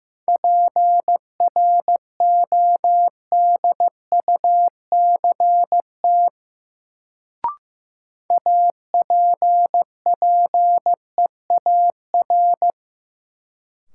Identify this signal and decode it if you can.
Morse code, heard as a keyed tone, about 15 wpm, 700 Hz. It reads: PRODUCT APPEAR
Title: Sentences from Top 1000 Words - Review Part 2 - New Word Drill - 15wpm